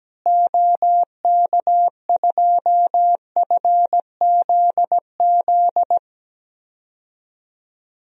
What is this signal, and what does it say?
Morse code, heard as a keyed tone, about 17 wpm, 700 Hz. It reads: OK2FZZ
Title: DXCC-100 Call Signs - 6 Characters - Repeated 3x - 17wpm